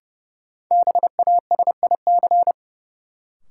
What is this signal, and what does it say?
Morse code, heard as a keyed tone, about 29 wpm, 700 Hz. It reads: BASIC
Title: Sentences from Top 1000 Words - Review Part 3 - New Word Drill - 30wpm